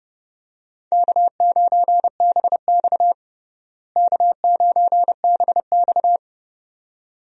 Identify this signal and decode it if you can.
Morse code, heard as a keyed tone, about 30 wpm, 700 Hz. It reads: K9BX K9BX
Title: USA Call Signs - 4 Characters - Repeated 2x - 30wpm